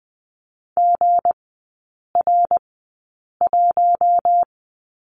Morse code, heard as a keyed tone, about 20 wpm, spaced as for 8 wpm, 690 Hz. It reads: GR1
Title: 159 Lesson 26e - Letter g - Three Characters 20wpm